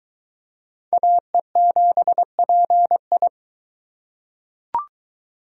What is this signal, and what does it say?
Morse code, heard as a keyed tone, about 20 wpm, 700 Hz. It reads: AE7PI